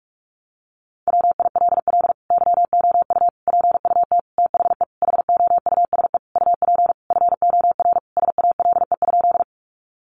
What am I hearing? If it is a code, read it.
Morse code, heard as a keyed tone, about 44 wpm, 700 Hz. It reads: WILL YOU PUT THE HOUSE UP FOR SALE?
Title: Sentences from Top 1400 Words 45wpm